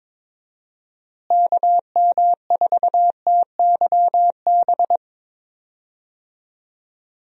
Morse code, heard as a keyed tone, about 22 wpm, 700 Hz. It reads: KM4TYB
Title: USA Call Signs - 6 Characters - Repeated 2x - 22wpm